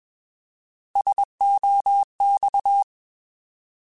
Morse code, heard as a keyed tone, about 21 wpm, 800 Hz. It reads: SOX
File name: Alphabetical / Alphabetical mp3s / Ss @20